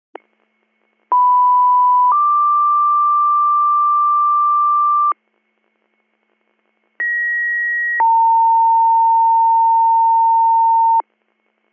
Two-tone page, heard as a one-second tone, then a three-second tone, about 990 then 1180 Hz, about 1900 then 910 Hz. It is heard from 1.1 to 11.0 s.